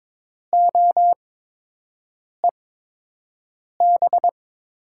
Morse code, heard as a keyed tone, about 22 wpm, spaced as for 6 wpm, 700 Hz. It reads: OEB
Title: USA Call Signs - 8x Character Spacing - 22wpm